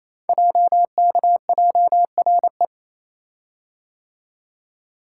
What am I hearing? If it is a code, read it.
Morse code, heard as a keyed tone, about 28 wpm, 700 Hz. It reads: JKJRE
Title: Code Groups - 5 Letters - 28wpm